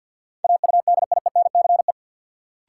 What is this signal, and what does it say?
Morse code, heard as a keyed tone, about 53 wpm, 700 Hz. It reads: AUDIENCE
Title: Sentences from Top 800 Words - New Word Drill - 50wpm